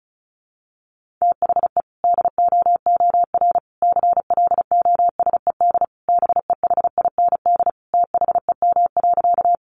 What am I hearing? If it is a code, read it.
Morse code, heard as a keyed tone, about 34 wpm, 700 Hz. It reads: THE DOOR CLOSED BEHIND THEM.
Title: Sentences from Top 1400 Words 35wpm